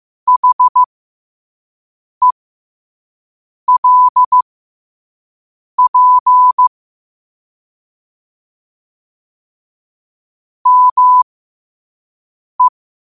Morse code, heard as a keyed tone, about 15 wpm, spaced as for 5 wpm, 1000 Hz. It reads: HELP ME
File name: qotdfiles / 5wpm